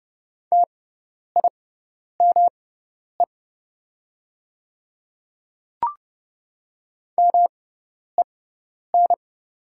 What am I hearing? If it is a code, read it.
Morse code, heard as a keyed tone, about 31 wpm, spaced as for 10 wpm, 700 Hz. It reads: TIME MEN
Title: 6x Character Spacing - Top 300 Words - 30wpm